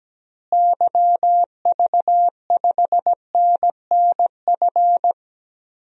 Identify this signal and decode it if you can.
Morse code, heard as a keyed tone, about 17 wpm, 700 Hz. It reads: YV5NNF
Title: DXCC-100 Call Signs - 6 Characters - Repeated 3x - 17wpm